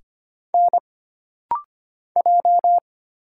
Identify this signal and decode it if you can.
Morse code, heard as a keyed tone, about 25 wpm, 700 Hz. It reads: N J
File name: Single Character (Letter:Number:Punctuation:Prosign) - Rapid-Fire - 25wpm